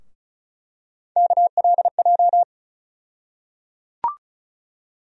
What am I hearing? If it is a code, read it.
Morse code, heard as a keyed tone, about 34 wpm, 700 Hz. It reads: KLJ
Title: USA Call Sign Postfixes - 3 Letters 35wpm